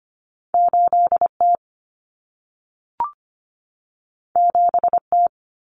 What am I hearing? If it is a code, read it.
Morse code, heard as a keyed tone, about 25 wpm, 700 Hz. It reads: 8T 7T
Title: Common Cut Numbers - 2-Digit 25wpm